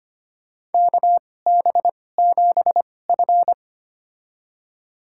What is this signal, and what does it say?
Morse code, heard as a keyed tone, about 25 wpm, spaced as for 18 wpm, 700 Hz. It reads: KB7F
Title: USA Call Signs - 4 Characters - 2x Character Spacing - 25wpm